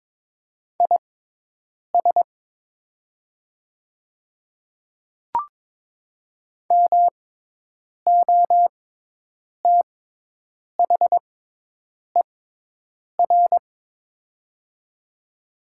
Morse code, heard as a keyed tone, about 21 wpm, spaced as for 8 wpm, 700 Hz. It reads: IS MOTHER
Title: 6x Character Spacing - Top 300 Words - 22wpm